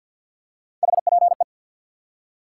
Morse code, heard as a keyed tone, about 52 wpm, 700 Hz. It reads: HPE
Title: Top 90 QSO Elements 50wpm